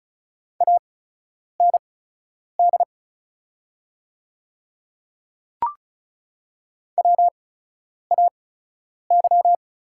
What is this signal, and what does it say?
Morse code, heard as a keyed tone, about 36 wpm, spaced as for 10 wpm, 700 Hz. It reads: AND WAY
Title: 8x Character Spacing - Top 100 Words - 35wpm